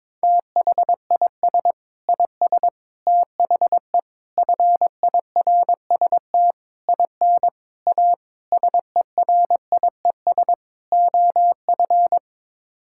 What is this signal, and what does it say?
Morse code, heard as a keyed tone, about 22 wpm, 700 Hz. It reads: THIS IS THE FIRST IN A SERIES OF
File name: practice / 22wpm / Sets of 8 Words - 22wpm